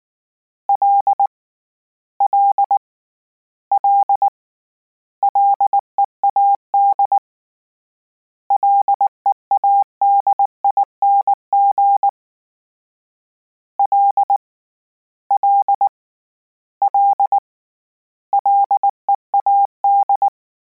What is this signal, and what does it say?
Morse code, heard as a keyed tone, about 19 wpm, 800 Hz. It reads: L L L LEAD LEADING L L L LEAD
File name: Combinations / Combinations-mp3s / Combinations@18